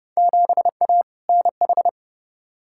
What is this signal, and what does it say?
Morse code, heard as a keyed tone, about 30 wpm, 700 Hz. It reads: 7A NH
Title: Field Day Exchanges - 30wpm